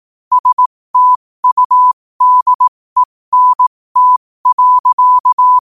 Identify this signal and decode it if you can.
Morse code, heard as a keyed tone, about 18 wpm, spaced as for 15 wpm, 1000 Hz. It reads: STUDENT.